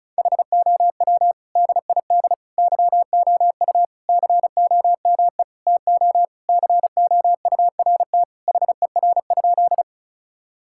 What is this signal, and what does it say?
Morse code, heard as a keyed tone, about 34 wpm, 700 Hz. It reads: HOW DID YOU COME TO COURT HER?
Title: Sentences from Top 1600 Words 35wpm